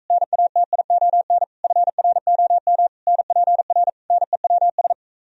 Morse code, heard as a keyed tone, about 45 wpm, 700 Hz. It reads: NATION FROM NPR NEWS